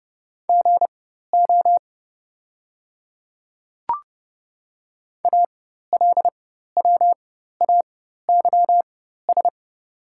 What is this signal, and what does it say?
Morse code, heard as a keyed tone, about 30 wpm, spaced as for 14 wpm, 700 Hz. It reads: GO ALWAYS
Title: 4x Character Spacing - Top 300 Words - 30wpm